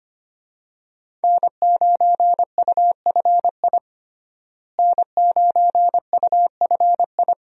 Morse code, heard as a keyed tone, about 25 wpm, 700 Hz. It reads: N9UFI N9UFI
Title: USA Call Signs - 5 Characters - Repeated 3x - 25wpm